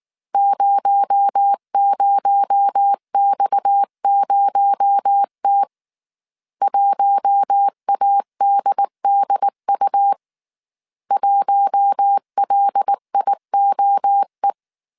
Morse code, heard as a keyed tone, about 19 wpm, 800 Hz. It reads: 00X0T 1ADDU 1LIOE
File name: CW / DET / 1 6 1500C S A 18wpm 2255